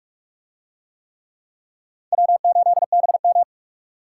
Morse code, heard as a keyed tone, about 47 wpm, 700 Hz. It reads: W8BM